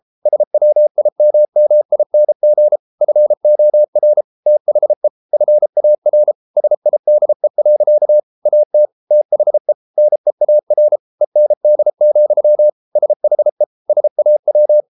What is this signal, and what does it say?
Morse code, heard as a keyed tone, about 33 wpm, 600 Hz. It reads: SWIMMING FOR THE FAR SIDE. AT THE NEAR END, SHE SAW